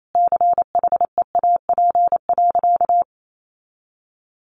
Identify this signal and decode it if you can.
Morse code, heard as a keyed tone, about 28 wpm, 700 Hz. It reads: CHEAP.
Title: Sentences from Top 1200 Words 28wpm